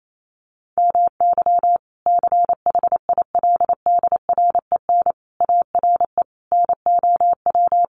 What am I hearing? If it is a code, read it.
Morse code, heard as a keyed tone, about 28 wpm, 700 Hz. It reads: MY CHILDREN ARE NOW
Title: Sentences from Top 1500 Words 28wpm